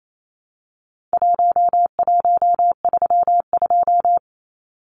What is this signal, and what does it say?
Morse code, heard as a keyed tone, about 28 wpm, 700 Hz. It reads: 1132
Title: Numbers - Contest Style - 4-Digit 28wpm